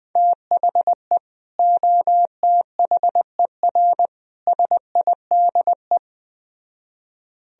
Morse code, heard as a keyed tone, about 20 wpm, 700 Hz. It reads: THE OTHER SIDE
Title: Speed-Racing - Sets of 3 Words 17wpm